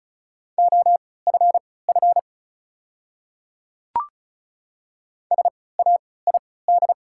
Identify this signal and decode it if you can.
Morse code, heard as a keyed tone, about 34 wpm, spaced as for 20 wpm, 700 Hz. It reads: OFF SAID
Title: 3x Character Spacing - Top 300 Words - 35wpm